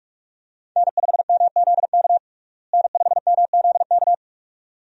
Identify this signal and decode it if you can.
Morse code, heard as a keyed tone, about 46 wpm, 700 Hz. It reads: N5MZK N5MZK
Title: USA Call Signs - 5 Characters - Repeated 2x - 45wpm